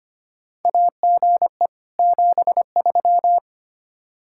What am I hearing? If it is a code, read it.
Morse code, heard as a keyed tone, about 25 wpm, 700 Hz. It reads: AGE 73
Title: Practice QSO - Exchange 3 - More Info - 25wpm